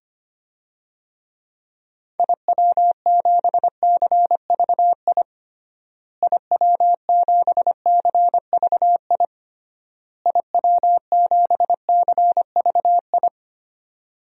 Morse code, heard as a keyed tone, about 25 wpm, 700 Hz. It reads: IW7CVI IW7CVI IW7CVI
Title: DXCC-100 Call Signs - 6 Characters - Repeated 3x - 25wpm